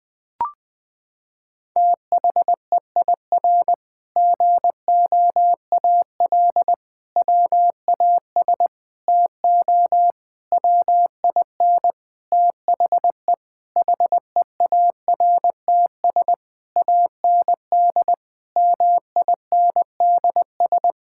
Morse code, heard as a keyed tone, about 20 wpm, 700 Hz. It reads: THEIR GOAL WAS TO WIN THE HEARTS AND MINDS
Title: Sentences with 8-Word Phrases - 20wpm